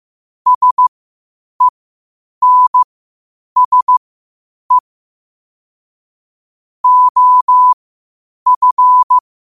Morse code, heard as a keyed tone, about 15 wpm, spaced as for 8 wpm, 1000 Hz. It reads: SENSE OF
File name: qotdfiles / 8wpm